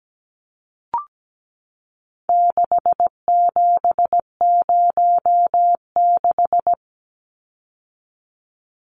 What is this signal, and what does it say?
Morse code, heard as a keyed tone, about 17 wpm, 700 Hz. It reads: 6706
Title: Numbers - 4-Digit 17wpm